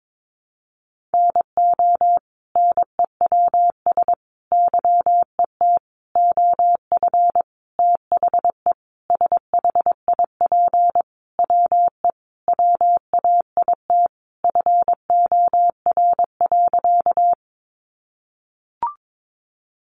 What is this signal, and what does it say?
Morse code, heard as a keyed tone, about 22 wpm, 700 Hz. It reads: NO NEWS YET OF THE SHIP WE WAIT FOR.
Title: Sentences from Top 800 Words 22wpm